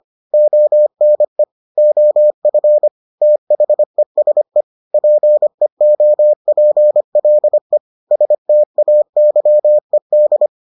Morse code, heard as a keyed tone, about 25 wpm, 600 Hz. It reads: ONE OF THESE PEOPLE STAYED